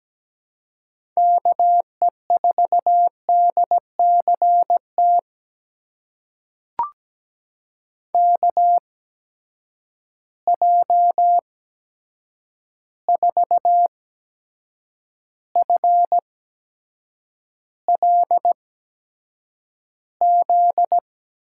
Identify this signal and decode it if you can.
Morse code, heard as a keyed tone, about 17 wpm, spaced as for 7 wpm, 700 Hz. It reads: KE4DCT KJ4FLZ
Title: USA Call Signs - 6 Characters - 8x Character Spacing - 17wpm